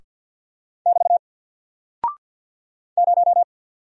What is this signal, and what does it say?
Morse code, heard as a keyed tone, about 48 wpm, 700 Hz. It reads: <BT> 0
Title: Single Character (Letter:Number:Punctuation:Prosign) Alt 50wpm